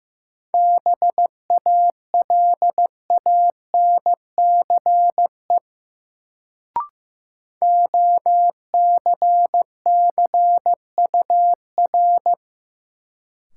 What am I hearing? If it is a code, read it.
Morse code, heard as a keyed tone, about 15 wpm, 700 Hz. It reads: BALANCE OCCUR
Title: Sentences from Top 900 Words - New Word Drill - 15wpm